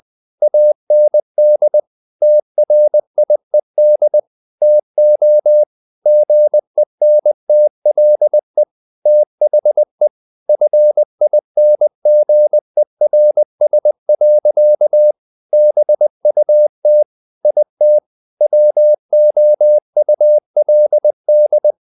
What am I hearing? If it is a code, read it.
Morse code, heard as a keyed tone, about 20 wpm, 600 Hz. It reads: AND TRIED TO GENTLE THE FINGERS. BUT IT WOULD
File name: the old man and the sea / the old man and the sea part 1 20wpm